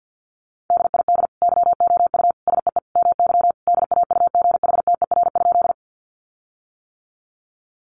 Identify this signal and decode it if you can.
Morse code, heard as a keyed tone, about 50 wpm, 700 Hz. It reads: DID YOU SEE MY DAUGHTER?
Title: Sentences from Top 1500 Words 50wpm